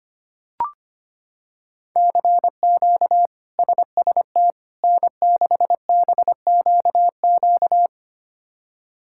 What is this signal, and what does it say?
Morse code, heard as a keyed tone, about 25 wpm, 700 Hz. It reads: CQ SST N6BQQ
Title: K1USN SST QSO Practice - 25wpm